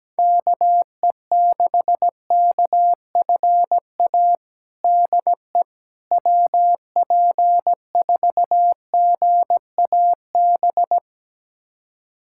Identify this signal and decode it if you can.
Morse code, heard as a keyed tone, about 17 wpm, 700 Hz. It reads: KE6KFA DE WP4GAB
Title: USA Callsign DE Callsign - 6 Characters - 17wpm